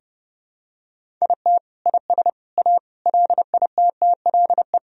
Morse code, heard as a keyed tone, about 32 wpm, 700 Hz. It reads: IT IS A LITTLE
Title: Sentences from Top 1000 Words - Review Part 1 - 30wpm